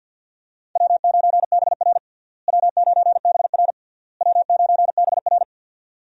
Morse code, heard as a keyed tone, about 48 wpm, 700 Hz. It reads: W9BR W9BR W9BR
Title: USA Call Signs - 4 Characters - Repeated 3x - 50wpm